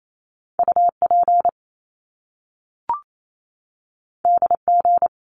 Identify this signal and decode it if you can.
Morse code, heard as a keyed tone, about 28 wpm, 700 Hz. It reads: U P D G
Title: Top Two Letter Combinations in Words 28wpm